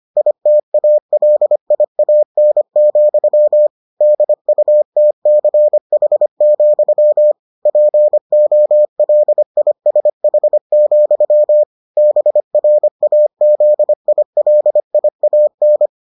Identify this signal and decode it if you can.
Morse code, heard as a keyed tone, about 25 wpm, 600 Hz. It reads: ITALIAN, DUTCH, POLISH, BRAZILIAN